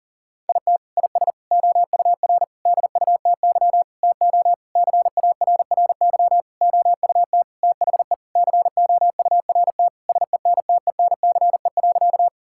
Morse code, heard as a keyed tone, about 40 wpm, 700 Hz. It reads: IT IS OUR DUTY TO CARRY OUT THE COURT SENTENCE.
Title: Sentences from Top 1600 Words 40wpm